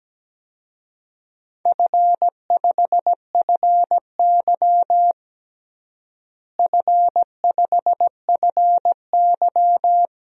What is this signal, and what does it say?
Morse code, heard as a keyed tone, about 17 wpm, 700 Hz. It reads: F5FY F5FY
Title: DXCC-100 Call Signs - 4 Characters - Repeated 2x - 17wpm